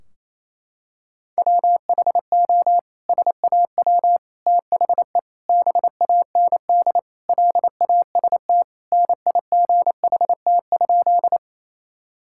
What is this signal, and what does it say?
Morse code, heard as a keyed tone, about 28 wpm, 700 Hz. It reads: WHO SAW THE BAND LAST NIGHT?
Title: Sentences from Top 1600 Words 28wpm